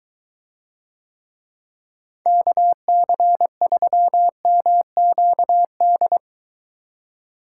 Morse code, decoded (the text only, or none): KC3MQD